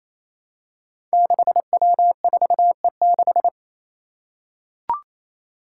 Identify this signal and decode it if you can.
Morse code, heard as a keyed tone, about 28 wpm, 700 Hz. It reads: 6W4E6
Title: Five Challenging Letter-Number Combinations 28wpm